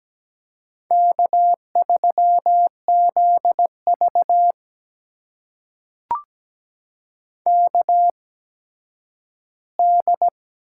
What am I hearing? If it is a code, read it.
Morse code, heard as a keyed tone, about 17 wpm, spaced as for 10 wpm, 700 Hz. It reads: K3ZV KD
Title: USA Call Signs - 4 Characters - 8x Character Spacing - 17wpm